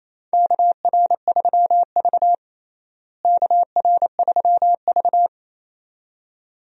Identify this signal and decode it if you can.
Morse code, heard as a keyed tone, about 28 wpm, 700 Hz. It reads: KR3V KR3V
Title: USA Call Signs - 4 Characters - Repeated 2x - 28wpm